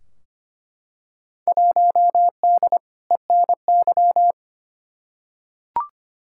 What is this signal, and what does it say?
Morse code, heard as a keyed tone, about 25 wpm, 700 Hz. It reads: 1D ENY
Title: Field Day Exchanges - 25wpm